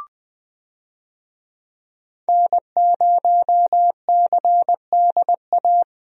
Morse code, heard as a keyed tone, about 21 wpm, 700 Hz. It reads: N0CDA
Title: USA Call Signs - 5 Characters - Repeated 3x - 20wpm